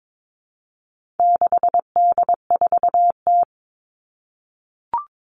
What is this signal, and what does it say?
Morse code, heard as a keyed tone, about 22 wpm, 700 Hz. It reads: 6D4T
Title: All Cut Numbers - 4-Digit 22wpm